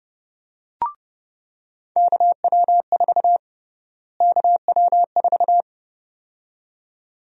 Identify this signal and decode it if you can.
Morse code, heard as a keyed tone, about 32 wpm, 700 Hz. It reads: KW4 KW4
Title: USA Call Sign Prefixes - Repeated 2x - 30wpm